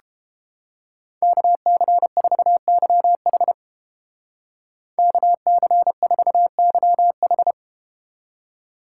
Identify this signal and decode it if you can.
Morse code, heard as a keyed tone, about 31 wpm, 700 Hz. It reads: KC4YH KC4YH